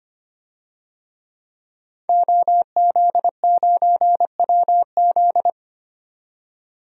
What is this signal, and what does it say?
Morse code, heard as a keyed tone, about 25 wpm, 700 Hz. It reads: OZ9WZ
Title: DXCC-100 Call Signs - 5 Characters - 25wpm